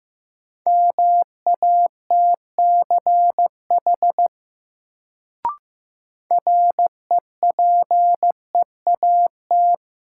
Morse code, heard as a keyed tone, about 15 wpm, 700 Hz. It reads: MATCH REPEAT
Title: Sentences from Top 1600 Words - New Word Drill Alt - 15wpm